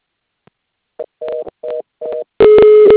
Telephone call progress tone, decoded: reorder tone